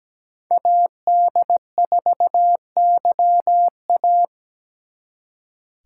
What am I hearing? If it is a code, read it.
Morse code, heard as a keyed tone, about 20 wpm, 700 Hz. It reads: AD4YA
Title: Speed Racing - USA 2x2 Call Signs 17wpm